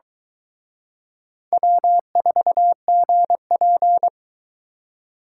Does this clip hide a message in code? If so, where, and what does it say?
Yes: Morse code from 1.5 to 4.1 s, heard as a keyed tone, about 23 wpm, 700 Hz. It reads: W4GP